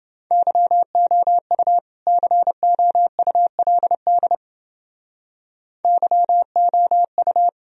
Morse code, heard as a keyed tone, about 29 wpm, 700 Hz. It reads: YOU COULD YOU